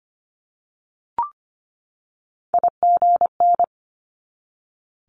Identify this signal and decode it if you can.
Morse code, heard as a keyed tone, about 25 wpm, 700 Hz. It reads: IGN
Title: Multi-letter Phonemes 25wpm